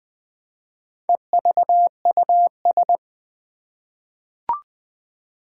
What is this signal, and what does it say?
Morse code, heard as a keyed tone, about 20 wpm, 700 Hz. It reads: EVUS
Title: Four Challenging Letter-Number Combinations 20wpm